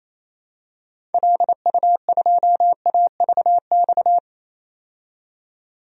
Morse code, heard as a keyed tone, about 28 wpm, 700 Hz. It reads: LU2AVX
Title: DXCC-100 Call Signs - 6 Characters - 28wpm